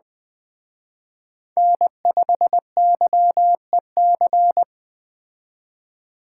Morse code, heard as a keyed tone, about 21 wpm, 700 Hz. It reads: N5YEC